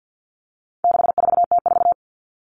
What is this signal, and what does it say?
Morse code, heard as a keyed tone, about 50 wpm, 700 Hz. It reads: 64T4
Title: All Cut Numbers - 4-Digit 50wpm